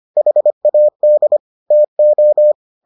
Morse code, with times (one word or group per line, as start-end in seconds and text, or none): 0.2-1.4 HAD
1.7-2.5 TO